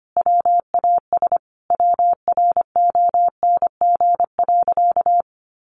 Morse code, heard as a keyed tone, about 25 wpm, 700 Hz. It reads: WAS WRONG.
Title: Sentences from Top 1100 Words 25wpm